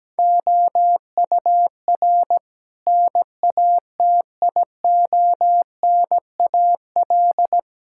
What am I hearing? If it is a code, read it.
Morse code, heard as a keyed tone, about 17 wpm, 700 Hz. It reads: OUR NATIONAL
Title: Sentences from Top 1000 Words - Review Part 6 - 17wpm